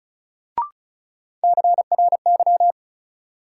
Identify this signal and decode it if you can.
Morse code, heard as a keyed tone, about 36 wpm, 700 Hz. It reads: CRY
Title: Sentences from Top 500 Words - New Word Drill - 35wpm